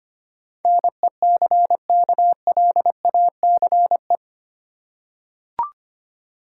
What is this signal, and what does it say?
Morse code, heard as a keyed tone, about 25 wpm, 700 Hz. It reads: NECKLACE
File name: practice / 25wpm / Easy Words with Hints - 25wpm